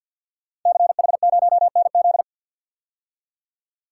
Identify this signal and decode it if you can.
Morse code, heard as a keyed tone, about 52 wpm, 700 Hz. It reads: CH0NZ
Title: Code Groups - 5 Letter:Numbers - 50wpm